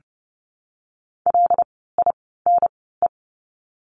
Morse code, heard as a keyed tone, about 30 wpm, spaced as for 17 wpm, 700 Hz. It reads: LINE
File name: course / 30:10wpm / 115 Lesson 19f - Letter w - Words 30wpm